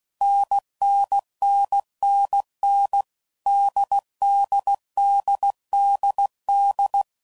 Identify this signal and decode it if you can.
Morse code, heard as a keyed tone, about 16 wpm, 800 Hz. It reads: NNNNN DDDDD
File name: Combinations / Difficult Combos / NDB NDB6